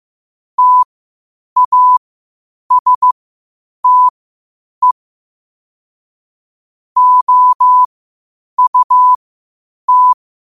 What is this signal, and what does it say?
Morse code, heard as a keyed tone, about 15 wpm, spaced as for 8 wpm, 1000 Hz. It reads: TASTE OUT
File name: qotdfiles / 8wpm